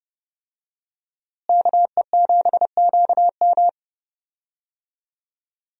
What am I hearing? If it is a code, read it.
Morse code, heard as a keyed tone, about 30 wpm, 700 Hz. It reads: KE7QM
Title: USA Call Signs - 5 Characters - Repeated 2x - 30wpm